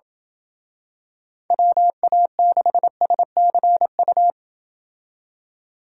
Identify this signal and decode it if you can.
Morse code, heard as a keyed tone, about 26 wpm, 700 Hz. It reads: WA6SCU